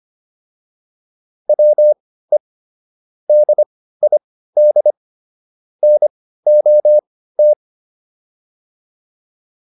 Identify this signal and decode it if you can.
Morse code, heard as a keyed tone, about 25 wpm, spaced as for 15 wpm, 600 Hz. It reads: WE DID NOT